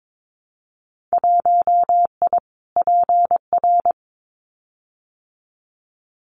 Morse code, heard as a keyed tone, about 22 wpm, 700 Hz. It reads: 1I PR
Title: Winter Field Day Exchanges Encore 22wpm